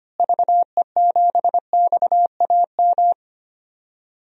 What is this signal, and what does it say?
Morse code, heard as a keyed tone, about 25 wpm, 700 Hz. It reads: VE7XAM
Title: DXCC-100 Call Signs - 6 Characters - Repeated 3x - 25wpm